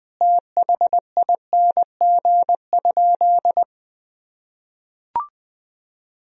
Morse code, heard as a keyed tone, about 20 wpm, 700 Hz. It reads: THING?
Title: Sentences from Top 200 Words and Abbreviations 20wpm